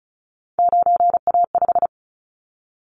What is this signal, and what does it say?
Morse code, heard as a keyed tone, about 34 wpm, 700 Hz. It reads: 9A5
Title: Callsign Prefixes for Top 25 DXCC - 35wpm